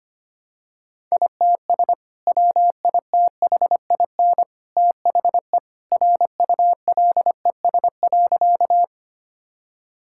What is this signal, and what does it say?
Morse code, heard as a keyed tone, about 25 wpm, 700 Hz. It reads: ITS WITHIN THE RULES.
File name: practice / 25wpm / Sentences with 2-Word Phrases - 25wpm